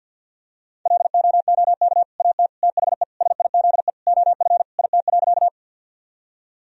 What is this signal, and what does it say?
Morse code, heard as a keyed tone, about 50 wpm, 700 Hz. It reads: LOOK AT THE SIZE OF IT.
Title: Sentences from Top 1000 Words - Review Part 2 - 50wpm